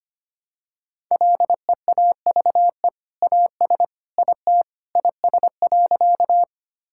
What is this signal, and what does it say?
Morse code, heard as a keyed tone, about 25 wpm, 700 Hz. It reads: LEAVE AS IT IS.
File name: practice / 25wpm / Sentences with 2-Word Phrases - 25wpm